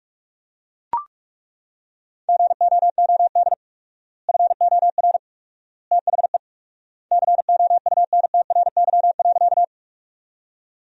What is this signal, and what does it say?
Morse code, heard as a keyed tone, about 43 wpm, 700 Hz. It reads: GOOD FOR THE COUNTRY.
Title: 4x Word-Spacing - Sentences from Top 300 Words - 45wpm